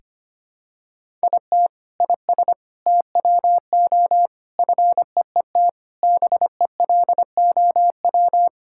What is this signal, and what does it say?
Morse code, heard as a keyed tone, about 25 wpm, 700 Hz. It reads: IT IS TWO FEET BELOW
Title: Sentences from Top 1000 Words - Review Part 5 - 25wpm